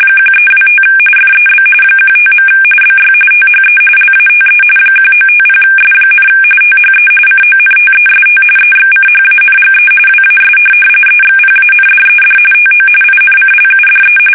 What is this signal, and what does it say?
RTTY, about 45.45 baud. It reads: 1325 FEB 15. BY THE LATE 1950S, THE CONTEST EXCHANGE WAS EXPANDED TO INCLUDE
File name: fsk850 / teletype-RTTY